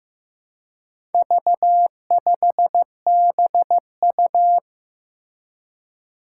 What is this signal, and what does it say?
Morse code, heard as a keyed tone, about 15 wpm, 700 Hz. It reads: V5BU
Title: Four Challenging Letter-Number Combinations 15wpm